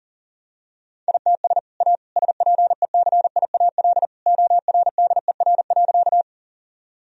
Morse code, heard as a keyed tone, about 40 wpm, 700 Hz. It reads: ITS A SPECIAL ORDER.
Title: Sentences from Top 1000 Words - Review Part 3 - 40wpm